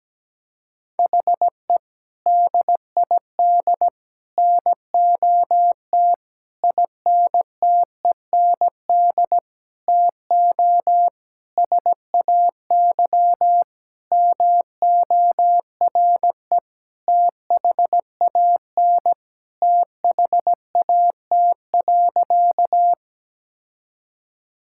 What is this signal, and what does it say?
Morse code, heard as a keyed tone, about 17 wpm, 700 Hz. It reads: HE DID NOT INTEND TO SAY MORE THAN THAT.
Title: Sentences from Top 1600 Words 17wpm